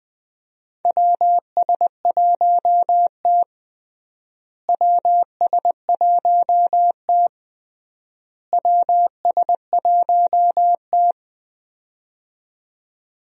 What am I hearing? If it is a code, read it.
Morse code, heard as a keyed tone, about 21 wpm, 700 Hz. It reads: WS1T WS1T WS1T
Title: USA Call Signs - 4 Characters - Repeated 3x - 20wpm